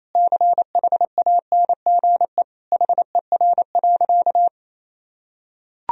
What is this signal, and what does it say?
Morse code, heard as a keyed tone, about 28 wpm, 700 Hz. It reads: CHANGE HER.
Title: Sentences from Top 1000 Words - Review Part 1 - 28wpm